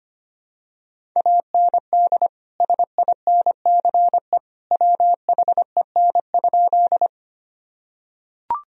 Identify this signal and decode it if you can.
Morse code, heard as a keyed tone, about 25 wpm, 700 Hz. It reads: AND SINCE WHEN?
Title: Sentences from Top 1000 Words - Review Part 1 - 25wpm